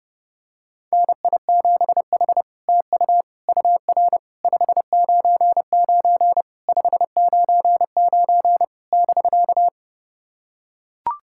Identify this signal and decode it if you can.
Morse code, heard as a keyed tone, about 30 wpm, 700 Hz. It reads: NI7H TU UR 599 599 <BK>
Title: SOTA QSO Practice 30wpm